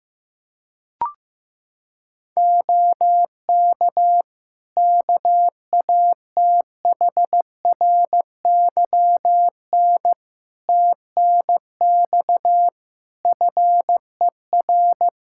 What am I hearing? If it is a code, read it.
Morse code, heard as a keyed tone, about 16 wpm, 700 Hz. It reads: OK KATHRYN TNX FER
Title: K1USN SST QSO Practice 15wpm